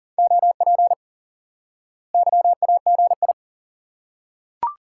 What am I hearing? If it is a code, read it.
Morse code, heard as a keyed tone, about 40 wpm, 700 Hz. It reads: OP YAGI
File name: IQE Course 3 - Lesson 89 - YAGI - Two Elements - 40wpm